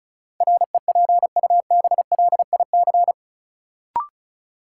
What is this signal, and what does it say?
Morse code, heard as a keyed tone, about 36 wpm, 700 Hz. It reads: REPUBLIC
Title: Sentences from Top 1500 Words - New Word Drill - 35wpm